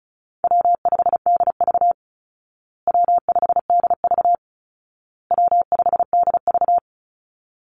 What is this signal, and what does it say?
Morse code, heard as a keyed tone, about 34 wpm, 700 Hz. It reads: W5DV W5DV W5DV
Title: Call Signs Repeated 2x - 35wpm